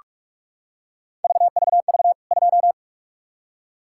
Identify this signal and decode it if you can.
Morse code, heard as a keyed tone, about 41 wpm, 700 Hz. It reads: VVV 2
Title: K1USN SST QSO Practice 45wpm